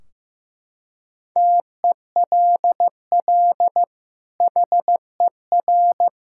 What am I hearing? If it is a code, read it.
Morse code, heard as a keyed tone, about 15 wpm, 700 Hz. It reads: TELL HER
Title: Sentences from Top 1600 Words 15wpm